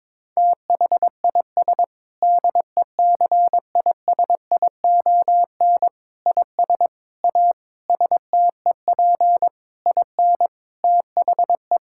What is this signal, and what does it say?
Morse code, heard as a keyed tone, about 22 wpm, 700 Hz. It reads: THIS DECISION IS A STEP IN THE
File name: Sentences with 7-Word Phrases - 22wpm